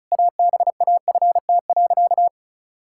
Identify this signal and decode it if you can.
Morse code, heard as a keyed tone, about 34 wpm, 700 Hz. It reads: ABAFT.
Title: Pangram Challenge v2 - 35wpm